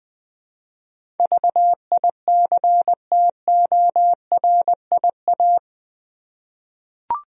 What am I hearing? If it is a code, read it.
Morse code, heard as a keyed tone, about 21 wpm, 700 Hz. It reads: VICTORIA
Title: USA Common Names - 20wpm